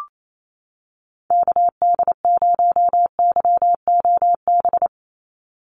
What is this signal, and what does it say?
Morse code, heard as a keyed tone, about 28 wpm, 700 Hz. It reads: KD0YOB
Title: NA SOTA Activator Callsigns 28wpm